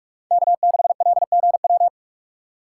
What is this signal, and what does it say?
Morse code, heard as a keyed tone, about 44 wpm, 700 Hz. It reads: K6LGW